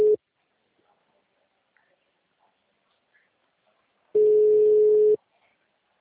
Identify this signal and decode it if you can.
Ringback tone (Europe, 425 Hz).